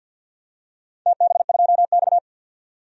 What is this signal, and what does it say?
Morse code, heard as a keyed tone, about 46 wpm, 700 Hz. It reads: TB2X
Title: DXCC-100 Call Signs - 4 Characters - 50wpm